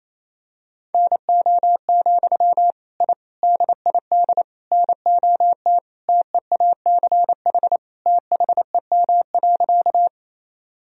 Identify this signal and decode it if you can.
Morse code, heard as a keyed tone, about 28 wpm, 700 Hz. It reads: NO, I DID NOT TEACH THEM.
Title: Sentences from Top 1000 Words - Review Part 2 - 28wpm